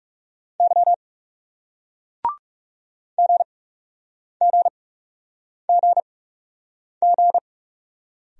Morse code, heard as a keyed tone, about 40 wpm, 700 Hz. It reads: Y G G G G